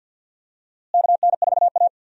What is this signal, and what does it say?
Morse code, heard as a keyed tone, about 50 wpm, 700 Hz. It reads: KN4A
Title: USA Call Signs - 4 Characters - Repeated 3x - 50wpm